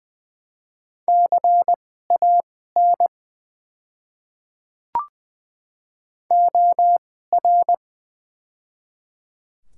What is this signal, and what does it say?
Morse code, heard as a keyed tone, about 20 wpm, spaced as for 15 wpm, 700 Hz. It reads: CAN OR
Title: 2x Character Spacing - Top 100 Words - 20wpm